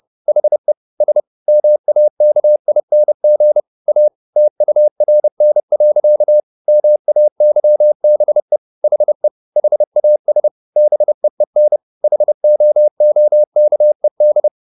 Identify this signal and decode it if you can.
Morse code, heard as a keyed tone, about 30 wpm, 600 Hz. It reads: HE S MAKING A TURN. MAYBE HE HAS BEEN HOOKED
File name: the old man and the sea / the old man and the sea 1 30wpm